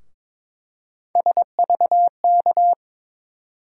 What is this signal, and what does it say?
Morse code, heard as a keyed tone, about 22 wpm, 700 Hz. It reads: SVK
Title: USA Call Sign Postfixes - 3 Letters 22wpm